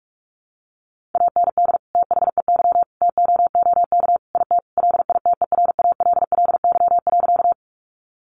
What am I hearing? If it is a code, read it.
Morse code, heard as a keyed tone, about 44 wpm, 700 Hz. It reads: AND THEY TOOK IT LITERALLY.
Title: Sentences from Top 1400 Words 45wpm